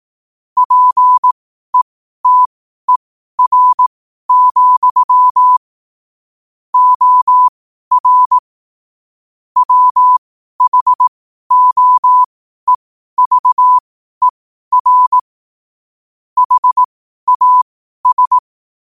Morse code, heard as a keyed tone, about 18 wpm, spaced as for 12 wpm, 1000 Hz. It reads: PETER, OR WHOEVER HAS
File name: qotdfiles / 12wpm